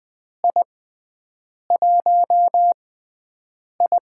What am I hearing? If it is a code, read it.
Morse code, heard as a keyed tone, about 20 wpm, spaced as for 7 wpm, 700 Hz. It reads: I1I
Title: USA Call Signs - 4 Characters - 6x Character Spacing - 20wpm